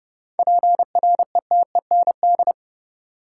Morse code, heard as a keyed tone, about 30 wpm, 700 Hz. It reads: PRETEND
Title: Seven Letter Words - 30wpm